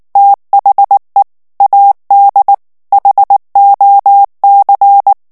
Morse code, heard as a keyed tone, about 19 wpm, 800 Hz. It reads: THE AD HOC
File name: ANGLAIS texte clair 3 vitesse 18 mots minute